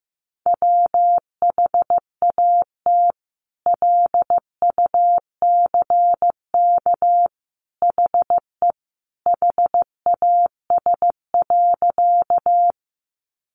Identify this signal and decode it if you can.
Morse code, heard as a keyed tone, about 15 wpm, 700 Hz. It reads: WHAT LUCK HE HAS.
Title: Sentences from Top 1500 Words 15wpm